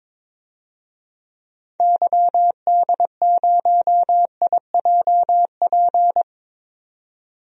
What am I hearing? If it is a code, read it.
Morse code, heard as a keyed tone, about 21 wpm, 700 Hz. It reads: YD0IJP
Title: DXCC-100 Call Signs - 6 Characters - 22wpm